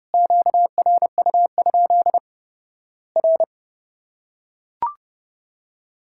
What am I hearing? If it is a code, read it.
Morse code, heard as a keyed tone, about 32 wpm, 700 Hz. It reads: QRU? R
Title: Worked All Europe DX Contest - QTC Exchange - 30wpm